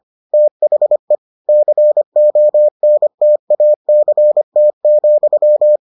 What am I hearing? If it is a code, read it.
Morse code, heard as a keyed tone, about 25 wpm, 600 Hz. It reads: THE CONTACT,